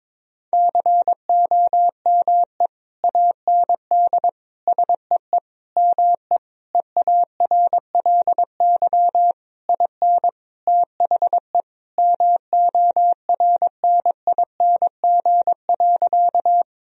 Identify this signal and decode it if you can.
Morse code, heard as a keyed tone, about 22 wpm, 700 Hz. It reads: COME AND SEE ME EARLY IN THE MORNING.
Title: Sentences from Top 1000 Words - Review Part 2 - 22wpm